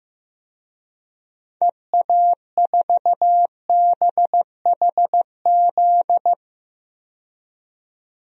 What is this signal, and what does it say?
Morse code, heard as a keyed tone, about 15 wpm, 700 Hz. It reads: EA4BHZ
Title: DXCC-100 Call Signs - 6 Characters - 15wpm